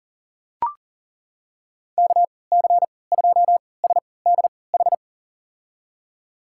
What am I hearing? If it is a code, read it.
Morse code, heard as a keyed tone, about 40 wpm, spaced as for 23 wpm, 700 Hz. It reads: KC2SDH
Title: USA Call Signs - 6 Characters - 3x Character Spacing - 40wpm